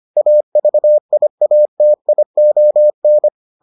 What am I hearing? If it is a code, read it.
Morse code, heard as a keyed tone, about 25 wpm, 600 Hz. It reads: AVIATION